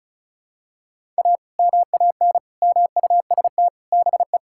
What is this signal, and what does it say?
Morse code, heard as a keyed tone, about 36 wpm, 700 Hz. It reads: A MAN MUST BE
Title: Sentences from Top 1000 Words - Review Part 6 - 35wpm